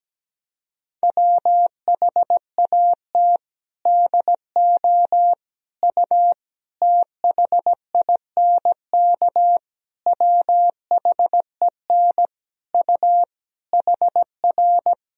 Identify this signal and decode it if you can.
Morse code, heard as a keyed tone, about 17 wpm, 700 Hz. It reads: WHAT DO U THINK WHEN U HR
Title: Sentences from Top 300 Words and Abbreviations 17wpm